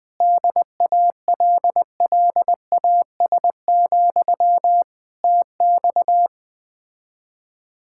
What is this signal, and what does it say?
Morse code, heard as a keyed tone, about 20 wpm, 700 Hz. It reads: DALLAS, TX
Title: Speed-Racing Common QSO Phrases - 17wpm